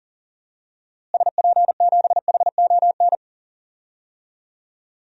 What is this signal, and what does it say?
Morse code, heard as a keyed tone, about 40 wpm, 700 Hz. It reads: SP7HON
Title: DXCC-100 Call Signs - 6 Characters - Repeated 2x - 40wpm